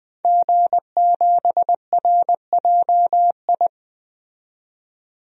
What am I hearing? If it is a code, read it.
Morse code, heard as a keyed tone, about 20 wpm, 700 Hz. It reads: G7RJI
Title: DXCC-100 Call Signs - 5 Characters - 20wpm